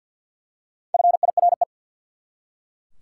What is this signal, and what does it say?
Morse code, heard as a keyed tone, about 52 wpm, 700 Hz. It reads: FIRE
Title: Four Letter Words - 50wpm